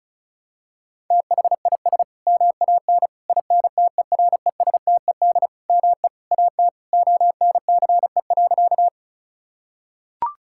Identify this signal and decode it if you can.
Morse code, heard as a keyed tone, about 34 wpm, 690 Hz. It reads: THIS MAN INTERESTED ME AT ONCE.
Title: Sentences from Top 1000 Words - Review Part 6 - 35wpm